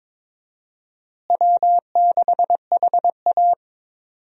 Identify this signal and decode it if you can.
Morse code, heard as a keyed tone, about 23 wpm, 700 Hz. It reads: W6HA
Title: USA Call Signs - 4 Characters - Repeated 3x - 22wpm